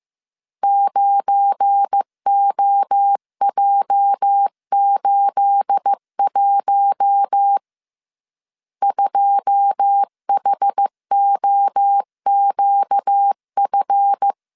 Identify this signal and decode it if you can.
Morse code, heard as a keyed tone, about 15 wpm, 800 Hz. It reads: 9OJ81 2HOQF
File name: CW / DET / 1 9 1500C S A 14wpm 3222